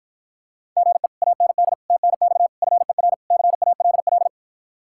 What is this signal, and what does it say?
Morse code, heard as a keyed tone, about 53 wpm, 700 Hz. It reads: GE AND TNX FER CALL